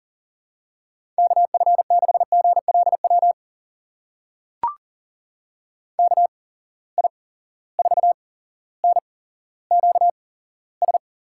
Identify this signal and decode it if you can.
Morse code, heard as a keyed tone, about 40 wpm, spaced as for 17 wpm, 700 Hz. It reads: KF6GLW KI4NQS
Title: USA Call Signs - 6 Characters - 8x Character Spacing - 40wpm